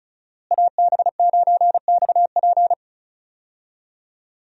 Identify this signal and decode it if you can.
Morse code, heard as a keyed tone, about 34 wpm, 700 Hz. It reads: AB9XP
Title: USA Call Signs - 5 Characters - Repeated 2x - 35wpm